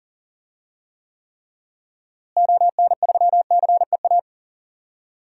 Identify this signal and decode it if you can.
Morse code, heard as a keyed tone, about 40 wpm, 700 Hz. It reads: ON3CEA
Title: DXCC-100 Call Signs - 6 Characters - Repeated 3x - 40wpm